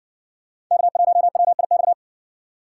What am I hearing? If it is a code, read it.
Morse code, heard as a keyed tone, about 59 wpm, 700 Hz. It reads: K1WIX